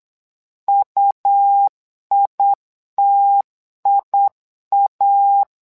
Morse code, heard as a keyed tone, about 8 wpm, 800 Hz. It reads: UITIA
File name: LEZIONI / 2 UHDN / 2 UHD 8wpm 2258